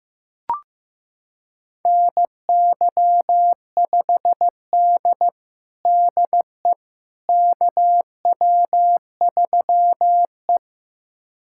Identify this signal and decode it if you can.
Morse code, heard as a keyed tone, about 15 wpm, 700 Hz. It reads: NY5D DE KW3E
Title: USA Callsign DE Callsign - 4 Characters - 15wpm